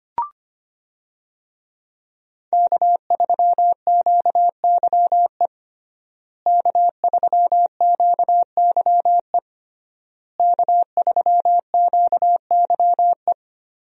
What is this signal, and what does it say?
Morse code, heard as a keyed tone, about 25 wpm, 700 Hz. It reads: K3QYE K3QYE K3QYE
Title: USA Call Signs - 5 Characters - Repeated 3x - 25wpm